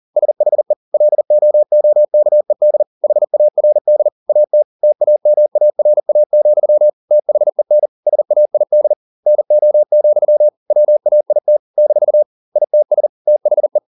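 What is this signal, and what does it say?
Morse code, heard as a keyed tone, about 40 wpm, 600 Hz. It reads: SHE LOOKED HARD AT TAMARA, THEN SAID NO, WAIT - ITS THE
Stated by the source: Morse code at 40 wpm, practice text, CW med Gnister.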